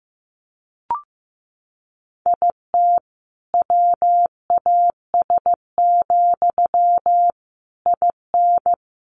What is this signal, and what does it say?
Morse code, heard as a keyed tone, about 15 wpm, 700 Hz. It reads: IT WAS, IN
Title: Sentences from Top 1400 Words 15wpm